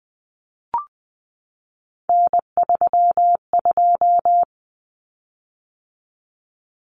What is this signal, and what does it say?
Morse code, heard as a keyed tone, about 20 wpm, 700 Hz. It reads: N32
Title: Common Cut Numbers - 3-Digit 20wpm